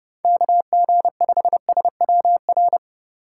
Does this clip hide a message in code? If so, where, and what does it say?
Yes: Morse code from 0.2 to 2.8 s, heard as a keyed tone, about 29 wpm, 700 Hz. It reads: KG5SWR